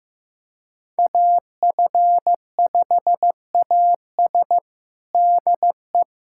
Morse code, heard as a keyed tone, about 15 wpm, 700 Hz. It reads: AF5AS DE
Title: USA Callsign DE Callsign - 5 Characters - 15wpm